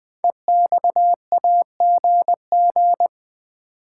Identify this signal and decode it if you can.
Morse code, heard as a keyed tone, about 20 wpm, 700 Hz. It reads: EXAGG
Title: Build A Word - 5 Letters At A Time - 10-Letter Words - 20wpm